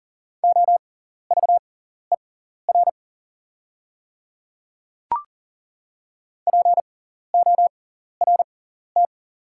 Morse code, heard as a keyed tone, about 40 wpm, spaced as for 14 wpm, 700 Hz. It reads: OVER PORT
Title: 6x Character Spacing - Top 200 Words - 40wpm